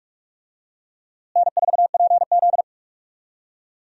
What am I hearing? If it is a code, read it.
Morse code, heard as a keyed tone, about 46 wpm, 700 Hz. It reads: N4PZ
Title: USA Call Signs - 4 Characters - Repeated 2x - 45wpm